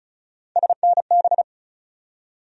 Morse code, heard as a keyed tone, about 36 wpm, 700 Hz. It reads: SNB
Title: USA Call Sign Postfixes - 3 Letters 35wpm